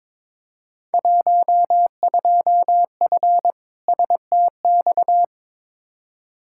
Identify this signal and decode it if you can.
Morse code, heard as a keyed tone, about 22 wpm, 700 Hz. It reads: 12F STX
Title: Field Day Exchanges Encore - 22wpm